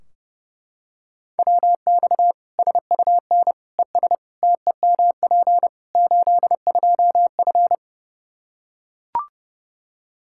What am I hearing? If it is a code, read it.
Morse code, heard as a keyed tone, about 30 wpm, 700 Hz. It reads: WX SUN ES TEMP 82F
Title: K1USN SST QSO Practice 30wpm